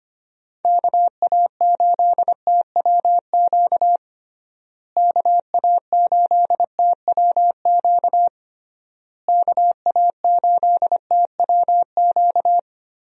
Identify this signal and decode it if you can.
Morse code, heard as a keyed tone, about 25 wpm, 700 Hz. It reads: KA8TWQ KA8TWQ KA8TWQ
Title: USA Call Signs - 6 Characters - Repeated 3x - 25wpm